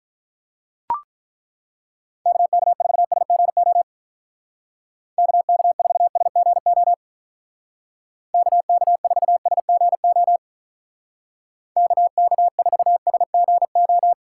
Morse code, heard as a keyed tone, about 43 wpm, 700 Hz. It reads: KK4SGO KK4SGO KK4SGO KK4SGO
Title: Speed Racing - USA 2x3 Call Signs 35wpm